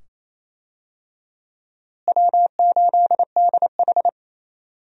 Morse code, heard as a keyed tone, about 28 wpm, 700 Hz. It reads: W8DH
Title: USA Call Signs - 4 Characters - Repeated 2x - 28wpm